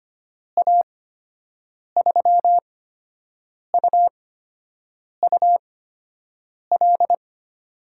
Morse code, heard as a keyed tone, about 25 wpm, spaced as for 7 wpm, 700 Hz. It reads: A3UUL
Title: USA Call Signs - 8x Character Spacing - 25wpm